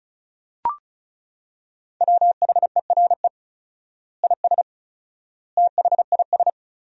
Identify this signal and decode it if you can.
Morse code, heard as a keyed tone, about 36 wpm, 700 Hz. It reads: WHERE IS THIS
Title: 4x Word-Spacing - Sentences from Top 300 Words - 35wpm